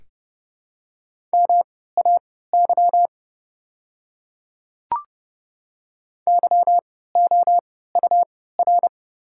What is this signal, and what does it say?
Morse code, heard as a keyed tone, about 30 wpm, spaced as for 17 wpm, 700 Hz. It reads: MAY YOUR
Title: 3x Character Spacing - Top 500 Words - 30wpm